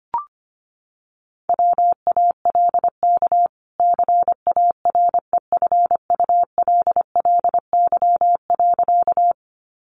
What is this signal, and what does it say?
Morse code, heard as a keyed tone, about 25 wpm, 700 Hz. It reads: WALK CAREFULLY.
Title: Sentences from Top 1300 Words 25wpm